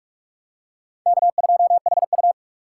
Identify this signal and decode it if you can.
Morse code, heard as a keyed tone, about 40 wpm, 700 Hz. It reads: K2HU